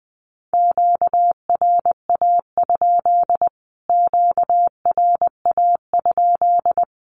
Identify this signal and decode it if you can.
Morse code, heard as a keyed tone, about 20 wpm, 700 Hz. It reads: QRA? QRA?
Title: Q-Codes Repeated 3x 20wpm